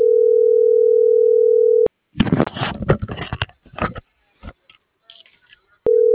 Ringback tone (North America), 440 + 480 Hz.